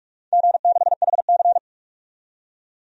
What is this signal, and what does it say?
Morse code, heard as a keyed tone, about 44 wpm, 700 Hz. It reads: G6HC
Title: DXCC-100 Call Signs - 4 Characters - 45wpm